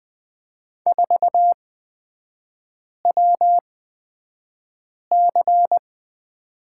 Morse code, heard as a keyed tone, about 20 wpm, spaced as for 5 wpm, 700 Hz. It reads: 4WC
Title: 250 Lesson 40e - BK - Three Characters 20wpm